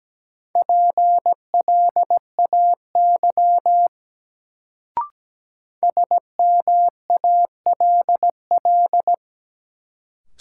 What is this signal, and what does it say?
Morse code, heard as a keyed tone, about 17 wpm, 700 Hz. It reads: PLAY SMALL
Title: Sentences from Top 200 Words - New Word Drill - 17wpm